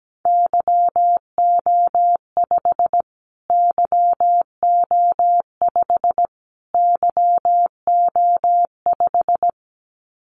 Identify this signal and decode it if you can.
Morse code, heard as a keyed tone, about 17 wpm, 700 Hz. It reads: YO5 YO5 YO5
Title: Callsign Prefixes for Top 25 DXCC - Repeated 3x - 17wpm